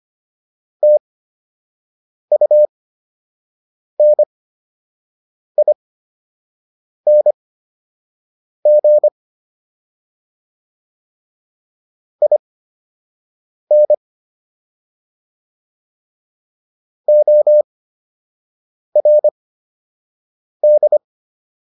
Morse code, heard as a keyed tone, about 25 wpm, spaced as for 6 wpm, 600 Hz. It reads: TUNING IN ORD